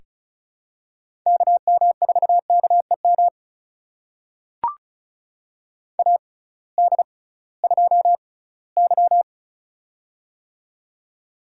Morse code, heard as a keyed tone, about 36 wpm, spaced as for 20 wpm, 700 Hz. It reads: KM4KEM AD2Y
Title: USA Call Signs - 6x Character Spacing - 35wpm